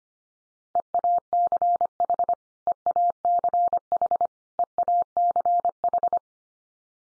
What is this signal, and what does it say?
Morse code, heard as a keyed tone, about 25 wpm, 700 Hz. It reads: EACH EACH EACH
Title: Top 100 Words 25wpm x3